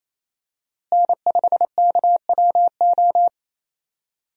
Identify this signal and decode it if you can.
Morse code, heard as a keyed tone, about 28 wpm, 700 Hz. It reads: N5KWO
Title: USA Call Signs - 5 Characters - Repeated 3x - 28wpm